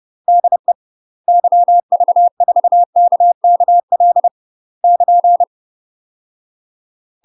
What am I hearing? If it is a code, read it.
Morse code, heard as a keyed tone, about 30 wpm, 700 Hz. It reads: DE YV4KKL <KN>
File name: UK MASTER QSOs TRAINING / 30WPM / UK MASTER QSOs 0008